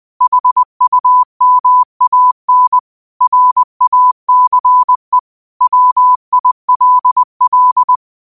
Morse code, heard as a keyed tone, about 20 wpm, 1000 Hz. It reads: HUMAN RACE WILL
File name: qotdfiles / 20wpm